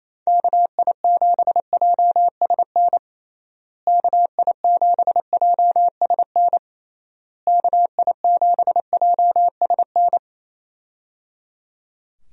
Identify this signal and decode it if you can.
Morse code, heard as a keyed tone, about 28 wpm, 700 Hz. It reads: KI7JSN KI7JSN KI7JSN
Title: USA Call Signs - 6 Characters - Repeated 3x - 28wpm